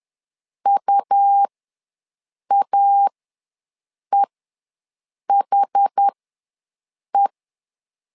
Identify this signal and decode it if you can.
Morse code, heard as a keyed tone, about 10 wpm, spaced as for 6 wpm, 800 Hz. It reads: UAEHE
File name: LEZIONI / 2 UHDN / 2 UHD ws 10wpm 2058